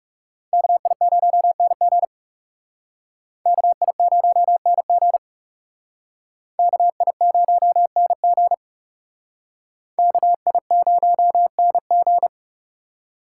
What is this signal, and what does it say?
Morse code, heard as a keyed tone, about 38 wpm, 700 Hz. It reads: KI0NG KI0NG KI0NG KI0NG